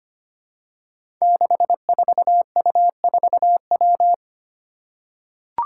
Morse code, heard as a keyed tone, about 25 wpm, 700 Hz. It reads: 64U4W
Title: Five Challenging Letter-Number Combinations 25wpm